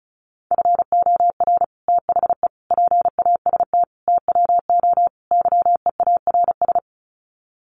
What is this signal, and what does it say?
Morse code, heard as a keyed tone, about 34 wpm, 700 Hz. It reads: FOR THE PAST TWO YEARS
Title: Sets of 5 Words Encore2 35wpm